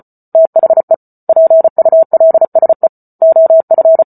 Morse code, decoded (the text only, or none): THE PULSE OF